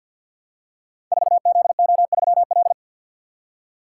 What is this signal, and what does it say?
Morse code, heard as a keyed tone, about 47 wpm, 700 Hz. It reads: 4ZO3L
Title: Code Groups - 5 Letter:Numbers - 50wpm